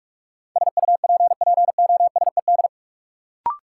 Morse code, heard as a keyed tone, about 45 wpm, 700 Hz. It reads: SUPPOSED
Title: Sets of 3 Words - New Word Drill - 45wpm